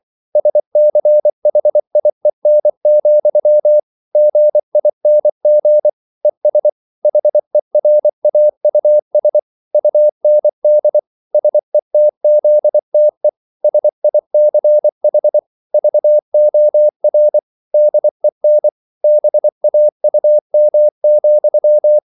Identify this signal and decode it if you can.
Morse code, heard as a keyed tone, about 24 wpm, 600 Hz. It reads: SCHIEN, GING ES HERAUS UND SETZTE SICH VOR DEN BAUM,